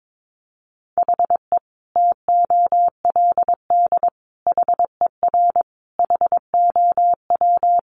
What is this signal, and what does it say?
Morse code, heard as a keyed tone, about 22 wpm, 700 Hz. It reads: HE TOLD HER HOW
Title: Sentences from Top 1500 Words 22wpm